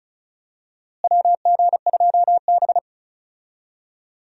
Morse code, heard as a keyed tone, about 36 wpm, 700 Hz. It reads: WG2B
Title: USA Call Signs - 4 Characters - Repeated 2x - 35wpm